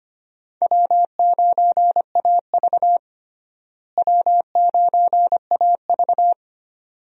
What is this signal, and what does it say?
Morse code, heard as a keyed tone, about 25 wpm, 700 Hz. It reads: W9AV W9AV
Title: USA Call Signs - 4 Characters - Repeated 2x - 25wpm